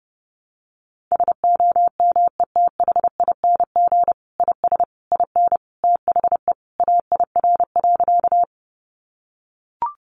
Morse code, heard as a keyed tone, about 30 wpm, 700 Hz. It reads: SOMETHING IS IN THE AIR.
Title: Sentences from Top 600 Words 30wpm